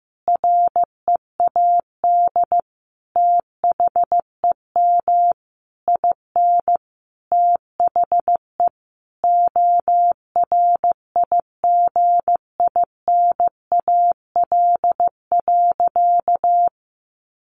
Morse code, heard as a keyed tone, about 15 wpm, 700 Hz. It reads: READ THEM IN THE ORIGINAL.
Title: Sentences from Top 900 Words 15wpm